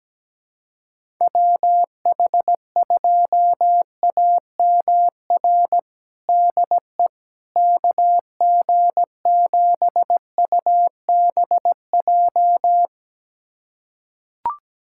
Morse code, heard as a keyed tone, about 17 wpm, 700 Hz. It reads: WH2AMR DE KG7UBJ
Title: USA Callsign DE Callsign - 6 Characters - 17wpm